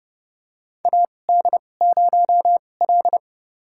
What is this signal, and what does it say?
Morse code, heard as a keyed tone, about 30 wpm, spaced as for 22 wpm, 700 Hz. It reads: AD0L
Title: USA Call Signs - 4 Characters - 2x Character Spacing - 30wpm